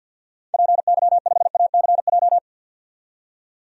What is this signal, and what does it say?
Morse code, heard as a keyed tone, about 51 wpm, 700 Hz. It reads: PY5ACJ
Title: DXCC-100 Call Signs - 6 Characters - 50wpm